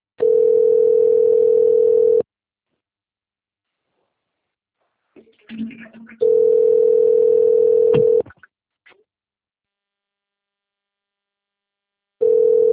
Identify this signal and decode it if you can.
Ringback tone (North America), 440 + 480 Hz.